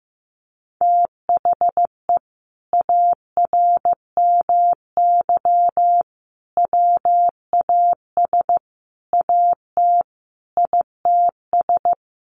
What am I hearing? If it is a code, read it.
Morse code, heard as a keyed tone, about 14 wpm, 700 Hz. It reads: THE ARMY WAS AT ITS
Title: Sentences from Top 1300 Words 15wpm